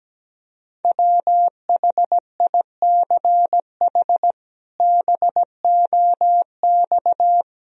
Morse code, heard as a keyed tone, about 17 wpm, 700 Hz. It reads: WHICH BOX
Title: Sentences from Top 500 Words and Abbreviations 17wpm